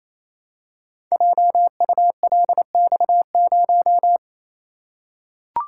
Morse code, heard as a keyed tone, about 27 wpm, 700 Hz. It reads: JULX0
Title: Code Groups - 5 Letter:Numbers - 28wpm